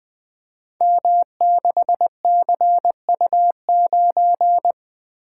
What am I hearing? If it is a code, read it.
Morse code, heard as a keyed tone, about 20 wpm, 700 Hz. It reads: M6CU9
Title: Code Groups - 5 Letter:Numbers - 20wpm